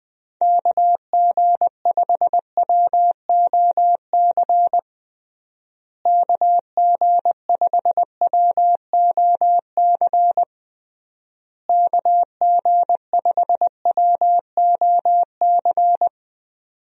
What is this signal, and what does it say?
Morse code, heard as a keyed tone, about 19 wpm, 700 Hz. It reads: KG5WOC KG5WOC KG5WOC
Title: USA Call Signs - 6 Characters - Repeated 3x - 20wpm